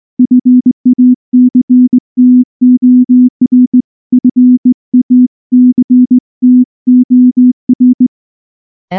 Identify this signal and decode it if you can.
Morse code, heard as a keyed tone, about 19 wpm, 250 Hz. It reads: FACTOR FACTOR
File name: morse / morse-training-20WPM-9